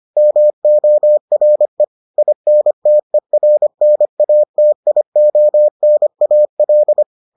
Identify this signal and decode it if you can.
Morse code, heard as a keyed tone, about 25 wpm, 600 Hz. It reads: MORE INTERNATIONAL